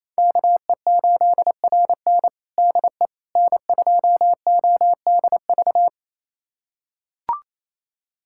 Morse code, heard as a keyed tone, about 27 wpm, 700 Hz. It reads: KE8RN DE N2ODV
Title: USA Callsign DE Callsign - 5 Characters - 28wpm